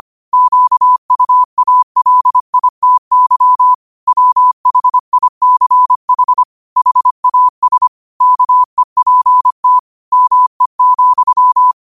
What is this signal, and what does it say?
Morse code, heard as a keyed tone, about 25 wpm, 1000 Hz. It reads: QUALITY WHICH HAS KEPT ME,
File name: qotdfiles / 25wpm